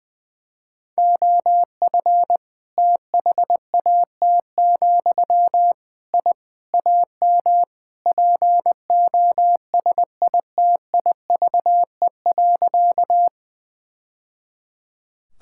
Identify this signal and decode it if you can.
Morse code, heard as a keyed tone, about 20 wpm, 700 Hz. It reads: OF THAT, I AM POSITIVE.
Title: Sentences from Top 1000 Words - Review Part 5 - 20wpm